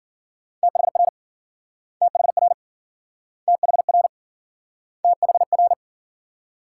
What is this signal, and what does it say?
Morse code, heard as a keyed tone, about 52 wpm, 700 Hz. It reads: THR THR THR THR